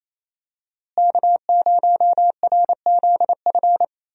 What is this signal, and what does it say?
Morse code, heard as a keyed tone, about 28 wpm, 700 Hz. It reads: K0RZF
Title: USA Call Signs - 5 Characters - Repeated 2x - 28wpm